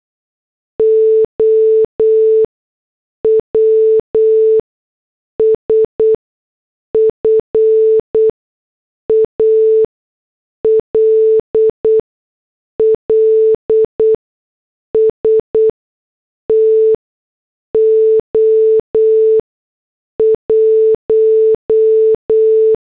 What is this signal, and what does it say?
Morse code, heard as a keyed tone, about 8 wpm, spaced as for 6 wpm, 440 Hz. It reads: OWS FALLS TO 1